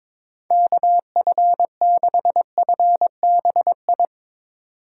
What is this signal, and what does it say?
Morse code, heard as a keyed tone, about 22 wpm, 700 Hz. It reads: KF6FBI
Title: USA Call Signs - 6x Character Spacing - 22wpm